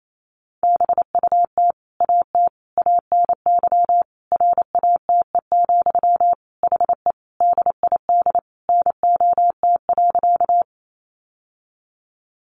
Morse code, heard as a keyed tone, about 28 wpm, 700 Hz. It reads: BUT AT ANY RATE, HE DID NOT.
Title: Sentences from Top 700 Words 28wpm